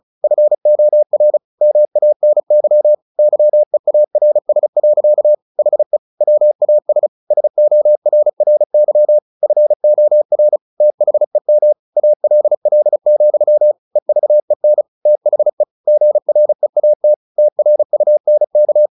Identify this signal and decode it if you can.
Morse code, heard as a keyed tone, about 34 wpm, 600 Hz. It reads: FOR MANY YEARS. HE WAS SORRY FOR THEM ALL, EVEN THE GREAT TRUNK
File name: the old man and the sea / the old man and the sea 1 35wpm